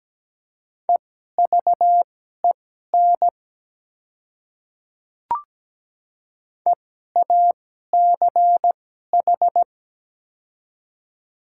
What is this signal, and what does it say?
Morse code, heard as a keyed tone, about 17 wpm, spaced as for 12 wpm, 690 Hz. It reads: EVEN EACH
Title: 2x Character Spacing - Top 300 Words - 17wpm